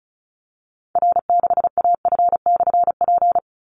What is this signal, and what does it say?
Morse code, heard as a keyed tone, about 34 wpm, 700 Hz. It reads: R6AF/P
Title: Call Signs Repeated 2x - 35wpm